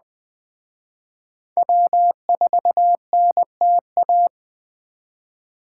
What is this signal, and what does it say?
Morse code, heard as a keyed tone, about 20 wpm, 700 Hz. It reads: W4NTA